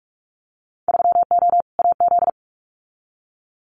Morse code, heard as a keyed tone, about 44 wpm, 700 Hz. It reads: 3O AZ
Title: Winter Field Day Exchanges 45wpm